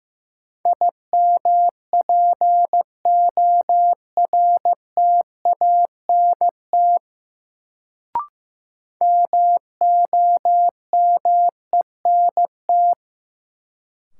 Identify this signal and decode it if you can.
Morse code, heard as a keyed tone, about 16 wpm, 700 Hz. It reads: IMPORTANT MOMENT
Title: Sets of 3 Words - New Word Drill - 15wpm